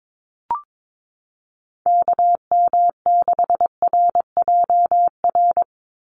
Morse code, heard as a keyed tone, about 22 wpm, 700 Hz. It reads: KM6RJR
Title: USA Call Signs 22wpm